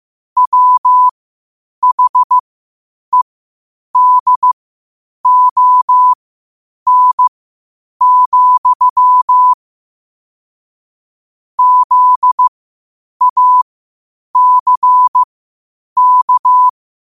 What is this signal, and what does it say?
Morse code, heard as a keyed tone, about 15 wpm, spaced as for 8 wpm, 1000 Hz. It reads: WHEDON, ZACK
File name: qotdfiles / 8wpm